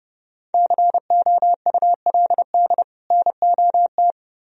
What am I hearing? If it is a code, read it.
Morse code, heard as a keyed tone, about 29 wpm, 700 Hz. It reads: COULD NOT